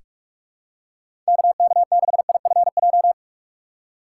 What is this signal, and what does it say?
Morse code, heard as a keyed tone, about 46 wpm, 700 Hz. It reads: KK6IFJ
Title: USA Call Signs - 2x Character Spacing - 45wpm